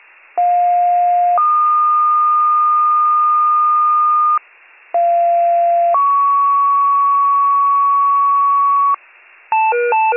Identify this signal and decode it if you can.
Two-tone page, heard as a one-second tone, then a three-second tone, about 710 then 1220 Hz, about 690 then 1120 Hz. It is heard from 0.4 to 9.0 s.